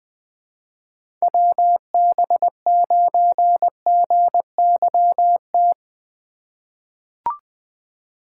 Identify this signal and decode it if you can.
Morse code, heard as a keyed tone, about 20 wpm, 700 Hz. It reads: WB9GYT
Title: USA Call Signs - 6 Characters - 3x Character Spacing - 20wpm